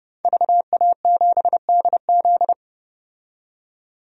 Morse code, heard as a keyed tone, about 30 wpm, 700 Hz. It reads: VA7DZ
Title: DXCC-100 Call Signs - 5 Characters - 30wpm